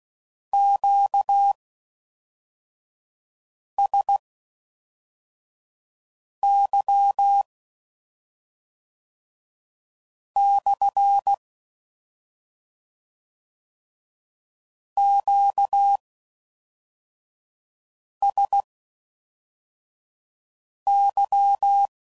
Morse code, heard as a keyed tone, about 16 wpm, spaced as for 3 wpm, 800 Hz. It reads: QSY/ QSY